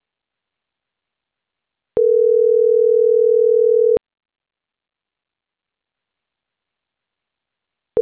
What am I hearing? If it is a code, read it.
Ringback tone (North America), 440 + 480 Hz.